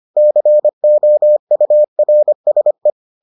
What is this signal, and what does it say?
Morse code, heard as a keyed tone, about 25 wpm, 600 Hz. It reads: COURSE